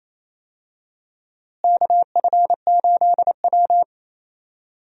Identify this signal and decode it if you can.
Morse code, heard as a keyed tone, about 27 wpm, 700 Hz. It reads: KF8W
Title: USA Call Signs - 4 Characters - Repeated 3x - 28wpm